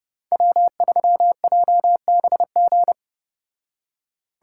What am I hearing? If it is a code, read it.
Morse code, heard as a keyed tone, about 31 wpm, 700 Hz. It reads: W3JBG